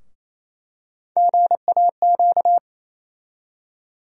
Morse code, heard as a keyed tone, about 28 wpm, 700 Hz. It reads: GAQ
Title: USA Call Sign Postfixes - 3 Letters 28wpm